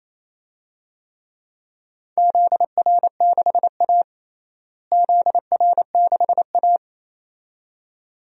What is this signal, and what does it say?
Morse code, heard as a keyed tone, about 27 wpm, 700 Hz. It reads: ZR6A ZR6A
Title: DXCC-100 Call Signs - 4 Characters - Repeated 2x - 28wpm